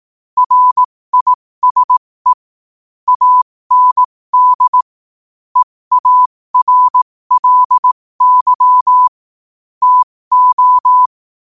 Morse code, heard as a keyed tone, about 18 wpm, spaced as for 15 wpm, 1000 Hz. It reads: RISE AND EARLY TO
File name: qotdfiles / 15wpm